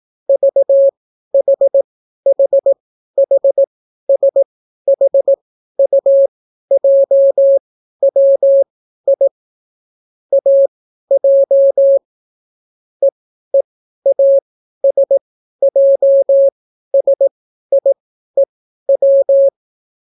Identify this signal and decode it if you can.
Morse code, heard as a keyed tone, about 18 wpm, spaced as for 12 wpm, 550 Hz. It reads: VHHHSHUJWI AJ EEASJSIEW